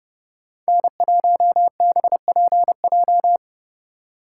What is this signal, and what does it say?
Morse code, heard as a keyed tone, about 31 wpm, 700 Hz. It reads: N1BPJ
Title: USA Call Signs - 2x Character Spacing - 30wpm